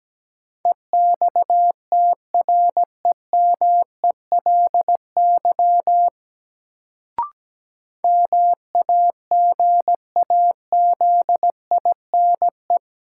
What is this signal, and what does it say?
Morse code, heard as a keyed tone, about 17 wpm, 700 Hz. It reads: EXTREMELY MAGAZINE
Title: Sentences from Top 1000 Words - Review Part 7 - New Word Drill - 17wpm